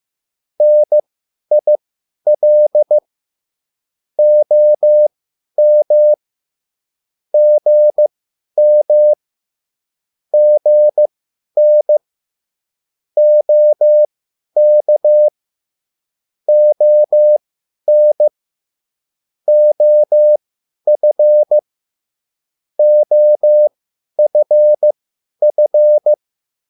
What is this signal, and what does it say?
Morse code, heard as a keyed tone, about 15 wpm, spaced as for 10 wpm, 600 Hz. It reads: NIL OM GM GN OK ON OF OFF
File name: cwbook